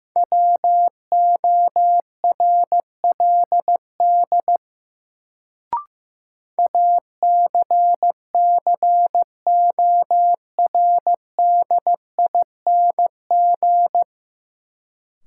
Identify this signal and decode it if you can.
Morse code, heard as a keyed tone, about 15 wpm, 700 Hz. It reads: WORLD ACCORDING
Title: Sets of 2 Words - New Word Drill - 15wpm